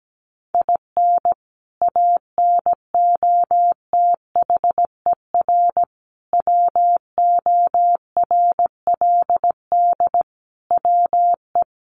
Morse code, heard as a keyed tone, about 17 wpm, 700 Hz. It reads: IN ANOTHER WORLD WE
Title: Sentences from Top 1100 Words 17wpm